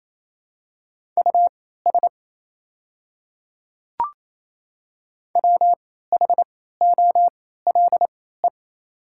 Morse code, heard as a keyed tone, about 28 wpm, spaced as for 16 wpm, 690 Hz. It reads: US WHOLE
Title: 3x Character Spacing - Top 500 Words - 28wpm